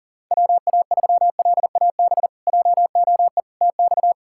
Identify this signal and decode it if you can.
Morse code, heard as a keyed tone, about 40 wpm, 700 Hz. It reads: WA3LAB JOE TX
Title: K1USN SST QSO Practice - 40wpm